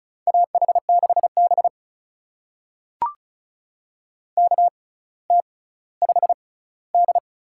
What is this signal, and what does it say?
Morse code, heard as a keyed tone, about 36 wpm, spaced as for 18 wpm, 700 Hz. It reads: AH6B KT5D
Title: USA Call Signs - 6x Character Spacing - 35wpm